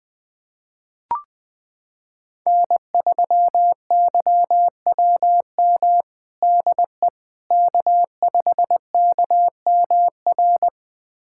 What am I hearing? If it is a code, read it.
Morse code, heard as a keyed tone, about 21 wpm, 700 Hz. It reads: N3YWM DE K5KMR
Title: USA Callsign DE Callsign - 5 Characters - 20wpm